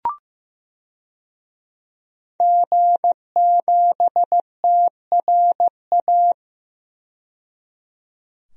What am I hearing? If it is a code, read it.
Morse code, heard as a keyed tone, about 15 wpm, 700 Hz. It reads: G7TRA
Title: DXCC-100 Call Signs - 5 Characters - 15wpm